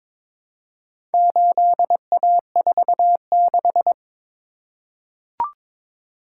Morse code, heard as a keyed tone, about 22 wpm, 700 Hz. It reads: 8A46
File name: practice / 22wpm / Four Letter-Number Combinations - 22wpm